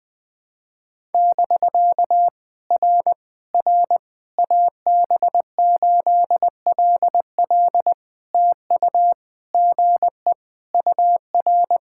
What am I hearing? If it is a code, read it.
Morse code, heard as a keyed tone, about 21 wpm, 700 Hz. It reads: <BK> R R AB8LL TU GE UR
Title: POTA QSO Practice - 20wpm